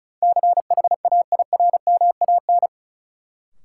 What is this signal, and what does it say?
Morse code, heard as a keyed tone, about 36 wpm, 700 Hz. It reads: CHAIRMAN
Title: Sets of 3 Words - New Word Drill - 35wpm